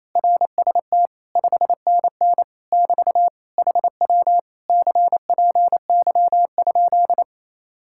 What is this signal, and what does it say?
Morse code, heard as a keyed tone, about 27 wpm, 700 Hz. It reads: RST 5NN <BT> HW CPY?
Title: Speed-Racing Common QSO Phrases Encore - 28wpm